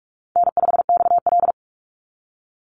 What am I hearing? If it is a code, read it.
Morse code, heard as a keyed tone, about 44 wpm, 700 Hz. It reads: N5XL
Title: NA SOTA Chaser Callsigns 45wpm